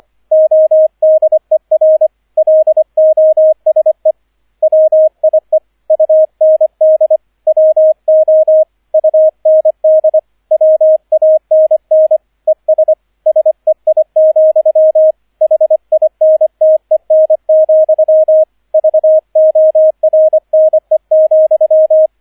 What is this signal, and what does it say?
Morse code, heard as a keyed tone, about 25 wpm, 630 Hz, with radio noise. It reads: ODER LOSE WIE UND WO UND WANN ES SEI, HINTEN, VORNE,